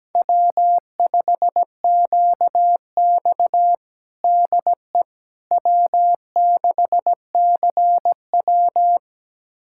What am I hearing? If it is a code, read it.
Morse code, heard as a keyed tone, about 17 wpm, 700 Hz. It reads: W5QX DE W6CW
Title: USA Callsign DE Callsign - 4 Characters - 17wpm